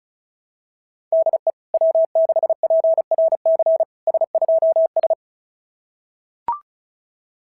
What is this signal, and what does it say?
Morse code, heard as a keyed tone, about 36 wpm, 660 Hz. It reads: DE W6PRC S2S
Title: SOTA QSO Practice 35wpm